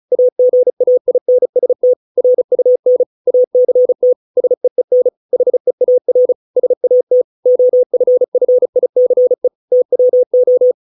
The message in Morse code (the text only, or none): AGAINST RUN ACT SEEN HEAR SAT OFFICE TWO